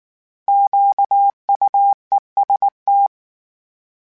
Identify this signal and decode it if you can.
Morse code, heard as a keyed tone, about 19 wpm, 800 Hz. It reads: QUEST